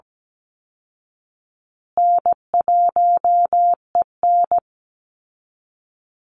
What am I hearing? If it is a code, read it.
Morse code, heard as a keyed tone, about 17 wpm, 700 Hz. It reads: N1EN
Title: Easy Call Signs Repeated 3x - 17wpm